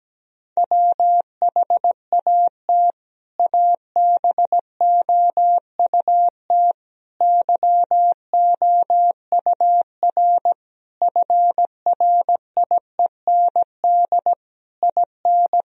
Morse code, heard as a keyed tone, about 17 wpm, 700 Hz. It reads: WHAT ABOUT YOUR FRIEND IN
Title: Sentences from Top 1000 Words - Review Part 1 - 17wpm